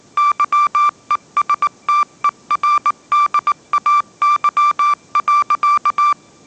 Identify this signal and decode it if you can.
Morse code, heard as a keyed tone, about 19 wpm, 1180 Hz. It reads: YESTERDAY.